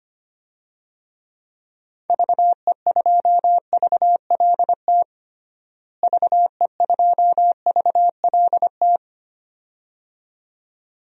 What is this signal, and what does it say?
Morse code, heard as a keyed tone, about 25 wpm, 700 Hz. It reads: VE2VLT VE2VLT
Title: DXCC-100 Call Signs - 6 Characters - Repeated 2x - 25wpm